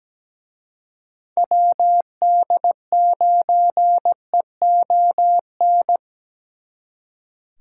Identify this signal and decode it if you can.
Morse code, heard as a keyed tone, about 20 wpm, 700 Hz. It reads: WD9EON
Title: Speed Racing - USA Call Signs 17wpm